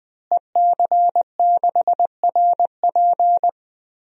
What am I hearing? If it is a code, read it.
Morse code, heard as a keyed tone, about 20 wpm, 700 Hz. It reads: EC6RP
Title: DXCC-100 Call Signs - 5 Characters - Repeated 3x - 20wpm